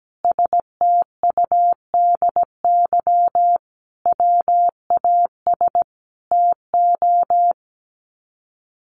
Morse code, heard as a keyed tone, about 17 wpm, 700 Hz. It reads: STUDY WAS TO
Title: Sets of 5 Words 17wpm